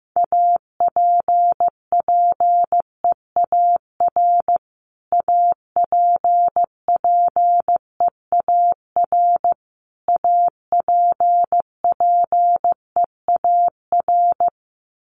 Morse code, heard as a keyed tone, about 15 wpm, 700 Hz. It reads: APPEAR APPEAR APPEAR
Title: Top 500 Words Repeated 15wpm